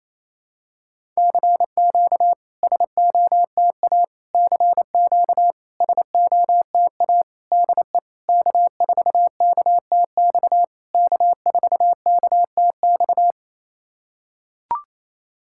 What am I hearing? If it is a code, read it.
Morse code, heard as a keyed tone, about 26 wpm, 700 Hz. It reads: CQ SOTA CQ SOTA DE K4KTX K4KTX
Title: SOTA QSO Practice 28wpm